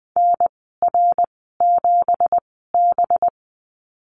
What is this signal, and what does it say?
Morse code, heard as a keyed tone, about 20 wpm, spaced as for 14 wpm, 700 Hz. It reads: NR7B
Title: 186 Lesson 30g - Letter b - Call Signs 20wpm